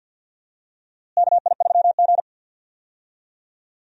Morse code, heard as a keyed tone, about 47 wpm, 700 Hz. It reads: KI3G
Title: USA Call Signs - 4 Characters - Repeated 3x - 50wpm